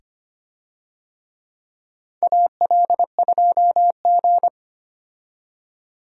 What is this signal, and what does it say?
Morse code, heard as a keyed tone, about 25 wpm, 700 Hz. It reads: AL2G
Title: USA Call Signs - 4 Characters - 25wpm